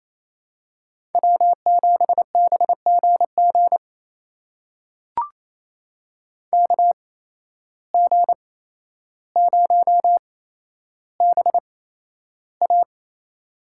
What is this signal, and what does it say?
Morse code, heard as a keyed tone, about 27 wpm, spaced as for 12 wpm, 700 Hz. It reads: W7BGG KG0BA
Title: USA Call Signs - 5 Characters - 8x Character Spacing - 28wpm